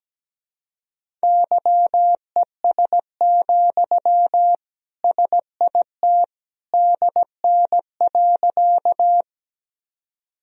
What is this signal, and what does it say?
Morse code, heard as a keyed tone, about 17 wpm, 700 Hz. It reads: YES, SIT DN.
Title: Sentences from Top 500 Words and Abbreviations 17wpm